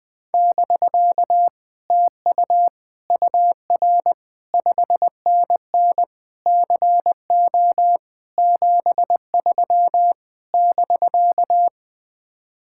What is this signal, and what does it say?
Morse code, heard as a keyed tone, about 20 wpm, 700 Hz. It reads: <BK> TU UR 5NN CO 73 <BK>